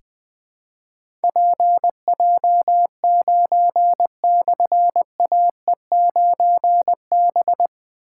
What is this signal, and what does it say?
Morse code, heard as a keyed tone, about 20 wpm, 700 Hz. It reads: PJ9/AE9B
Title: Difficult Call Signs 20wpm